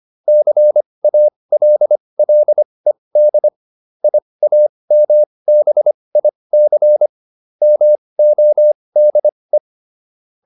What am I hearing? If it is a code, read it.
Morse code, heard as a keyed tone, about 25 wpm, spaced as for 20 wpm, 600 Hz. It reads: CALLED IAMBIC MODE
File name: SS215 20